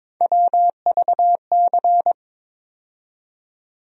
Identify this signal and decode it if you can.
Morse code, heard as a keyed tone, about 22 wpm, 700 Hz. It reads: WVC
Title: USA Call Sign Postfixes - 3 Letters 22wpm